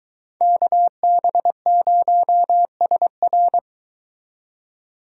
Morse code, heard as a keyed tone, about 22 wpm, 700 Hz. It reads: KB0SR